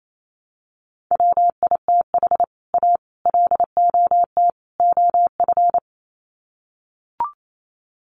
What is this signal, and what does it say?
Morse code, heard as a keyed tone, about 28 wpm, 700 Hz. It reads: WITH A LOT OF
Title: Sets of 4 Words 28wpm